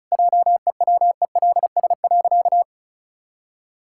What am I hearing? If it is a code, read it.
Morse code, heard as a keyed tone, about 36 wpm, 700 Hz. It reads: JEWELS.
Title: Pangram Challenge v2 - 35wpm